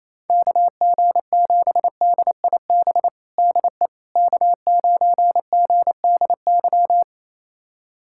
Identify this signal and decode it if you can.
Morse code, heard as a keyed tone, about 28 wpm, 700 Hz. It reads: KG7DIB DE K9GDY
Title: USA Callsign DE Callsign 28wpm